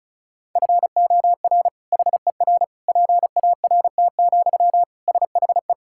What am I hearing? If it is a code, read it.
Morse code, heard as a keyed tone, about 36 wpm, 700 Hz. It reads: FOR HER PART, SHE
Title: Sentences from Top 1000 Words - Review Part 5 - 35wpm